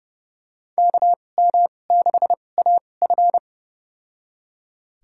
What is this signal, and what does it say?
Morse code, heard as a keyed tone, about 30 wpm, spaced as for 22 wpm, 700 Hz. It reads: KM6AF
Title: USA Call Signs - 5 Characters - 2x Character Spacing - 30wpm